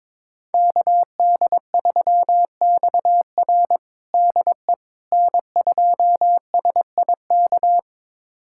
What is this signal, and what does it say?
Morse code, heard as a keyed tone, about 21 wpm, 700 Hz. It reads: KD3XR DE N2SIK
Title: USA Callsign DE Callsign - 5 Characters - 22wpm